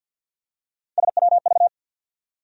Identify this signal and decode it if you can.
Morse code, heard as a keyed tone, about 47 wpm, 700 Hz. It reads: SWV
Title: USA Call Sign Postfixes - 3 Letters 50wpm